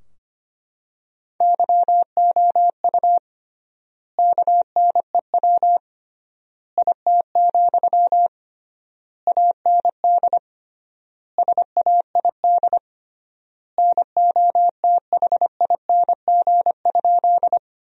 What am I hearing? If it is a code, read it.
Morse code, heard as a keyed tone, about 25 wpm, 700 Hz. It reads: YOU KNEW IT, AND SAID NOTHING?
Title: 3x Word-Spacing - Sentences from Top 500 Words - 25wpm